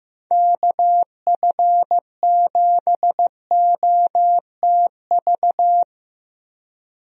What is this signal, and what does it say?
Morse code, heard as a keyed tone, about 15 wpm, 700 Hz. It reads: KF7OTV
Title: USA Call Signs - 6 Characters - Repeated 2x - 15wpm